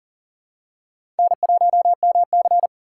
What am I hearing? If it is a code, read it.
Morse code, heard as a keyed tone, about 37 wpm, 700 Hz. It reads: N1MC
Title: USA Call Signs - 4x Character Spacing - 40wpm